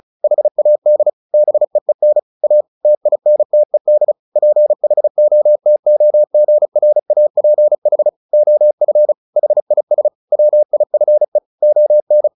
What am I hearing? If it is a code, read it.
Morse code, heard as a keyed tone, about 34 wpm, 600 Hz. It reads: HAD BEEN A TINTED PHOTOGRAPH OF HIS WIFE ON
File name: the old man and the sea / the old man and the sea 1 35wpm